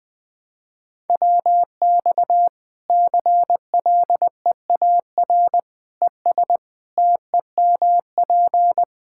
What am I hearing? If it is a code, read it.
Morse code, heard as a keyed tone, about 20 wpm, 700 Hz. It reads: WX CLEAR ES TEMP
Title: Practice QSO - Exchange 2 - Info - 20wpm